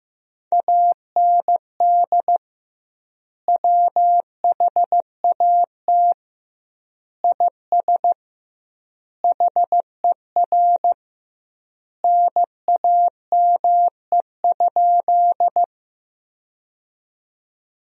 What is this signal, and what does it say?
Morse code, heard as a keyed tone, about 15 wpm, 700 Hz. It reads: AND WHAT IS HER NAME?
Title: 2x Word-Spacing - Sentences from Top 200 Words - 15wpm